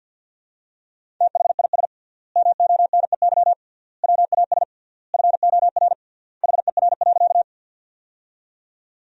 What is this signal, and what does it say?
Morse code, heard as a keyed tone, about 50 wpm, 700 Hz. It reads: THIS MONEY WAS FOR HER.
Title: 3x Word-Spacing - Sentences from Top 400 Words - 50wpm